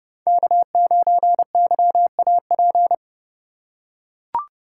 Morse code, heard as a keyed tone, about 28 wpm, 700 Hz. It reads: K9YAP